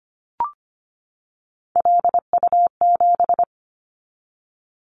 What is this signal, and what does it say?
Morse code, heard as a keyed tone, about 25 wpm, 700 Hz. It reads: LU7
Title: Callsign Prefixes for Top 50 DXCC - 25wpm